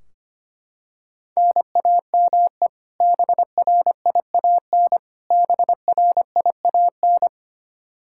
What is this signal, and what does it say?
Morse code, heard as a keyed tone, about 25 wpm, 700 Hz. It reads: NAME BRIAN BRIAN
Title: K1USN SST QSO Practice 25wpm